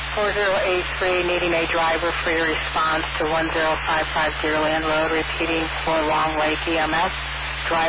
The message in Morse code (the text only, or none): none